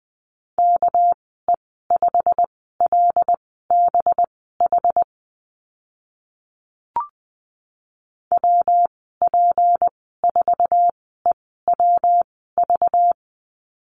Morse code, heard as a keyed tone, about 20 wpm, spaced as for 14 wpm, 700 Hz. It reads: KE5LBH WP4EWV
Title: 219 Lesson 35g - Number 0 - Call Signs 20wpm